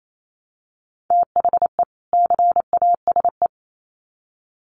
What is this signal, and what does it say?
Morse code, heard as a keyed tone, about 28 wpm, 700 Hz. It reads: THE CASE
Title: Sets of 2 Words 28wpm